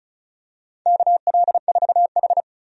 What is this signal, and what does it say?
Morse code, heard as a keyed tone, about 36 wpm, 700 Hz. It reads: KL4H
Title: USA Call Signs - 4 Characters - 35wpm